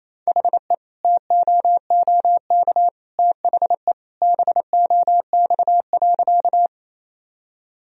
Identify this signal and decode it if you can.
Morse code, heard as a keyed tone, about 28 wpm, 700 Hz. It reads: HE TOOK THE BOX.
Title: Sentences from Top 1000 Words - Review Part 2 - 28wpm